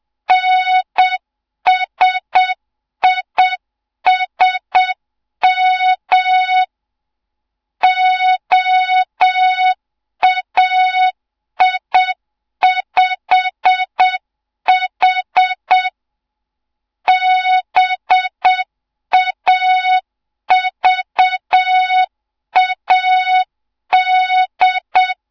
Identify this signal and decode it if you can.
Morse code, heard as a keyed tone, about 7 wpm, 3700 Hz. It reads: NSISM OAI5H BAVAD
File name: Morze-HA2MN / hang / 07 GYAKORLAT